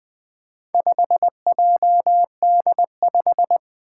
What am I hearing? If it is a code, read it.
Morse code, heard as a keyed tone, about 21 wpm, 700 Hz. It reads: 5JD5
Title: Four Challenging Letter-Number Combinations 20wpm